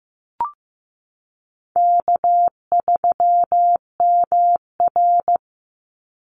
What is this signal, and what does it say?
Morse code, heard as a keyed tone, about 15 wpm, 700 Hz. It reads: K3MR
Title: Easy Call Signs Repeated 3x - 15wpm